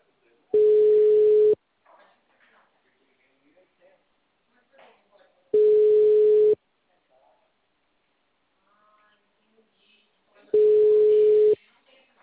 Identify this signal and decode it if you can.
Ringback tone (Europe, 425 Hz), with speech.
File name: out- -2011- - - .81